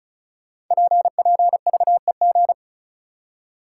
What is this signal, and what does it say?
Morse code, heard as a keyed tone, about 36 wpm, 700 Hz. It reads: PPVEG
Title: Code Groups - 5 Letter:Numbers - 35wpm